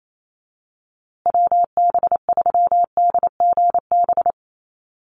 Morse code, heard as a keyed tone, about 28 wpm, 700 Hz. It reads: WB3DGB
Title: USA Call Signs Repeated 3x 28wpm